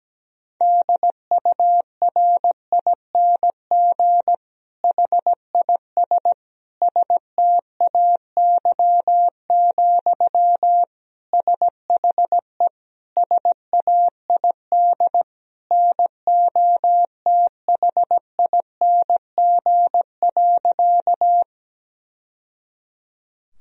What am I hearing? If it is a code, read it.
Morse code, heard as a keyed tone, about 17 wpm, 700 Hz. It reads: DURING HIS STAY, SHE SAID NOTHING.
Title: Sentences from Top 1000 Words - Review Part 2 - 17wpm